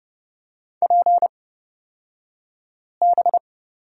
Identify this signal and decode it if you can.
Morse code, heard as a keyed tone, about 29 wpm, spaced as for 5 wpm, 700 Hz. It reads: PB
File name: Lesson 36d - BT - Two Characters 30wpm